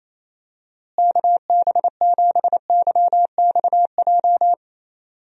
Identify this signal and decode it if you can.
Morse code, heard as a keyed tone, about 28 wpm, 700 Hz. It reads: KB7YXJ
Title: USA Call Signs - 6 Characters - Repeated 2x - 28wpm